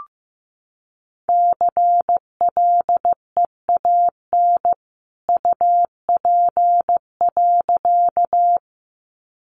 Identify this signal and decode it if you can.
Morse code, heard as a keyed tone, about 15 wpm, 700 Hz. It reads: CLEAN UP.
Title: A Sound Of Thunder 15wpm